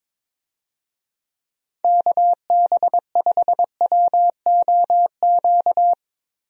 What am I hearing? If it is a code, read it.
Morse code, heard as a keyed tone, about 22 wpm, 700 Hz. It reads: KB5WOQ
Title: USA Call Signs - 6 Characters - Repeated 3x - 22wpm